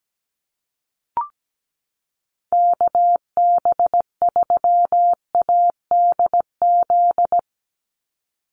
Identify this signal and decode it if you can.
Morse code, heard as a keyed tone, about 17 wpm, 700 Hz. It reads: KB3ADZ
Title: USA Call Signs 17wpm